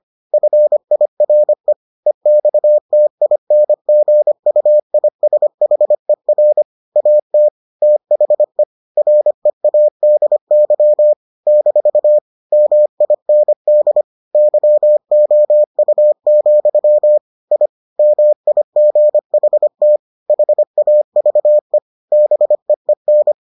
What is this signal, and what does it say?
Morse code, heard as a keyed tone, about 25 wpm, 600 Hz. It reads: FIRE EXTINGUISHER AT THE READY - MIND YOU, I MIGHT HAVE BEEN